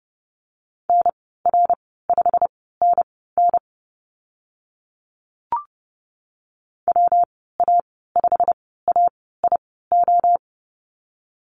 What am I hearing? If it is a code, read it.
Morse code, heard as a keyed tone, about 30 wpm, spaced as for 17 wpm, 700 Hz. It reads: NR5NN WA5AIO
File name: Lesson 19g - Letter w - Call Signs 30wpm